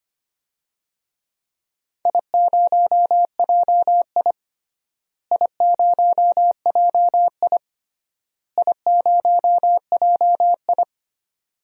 Morse code, heard as a keyed tone, about 25 wpm, 700 Hz. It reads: I0JI I0JI I0JI
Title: DXCC-100 Call Signs - 4 Characters - Repeated 3x - 25wpm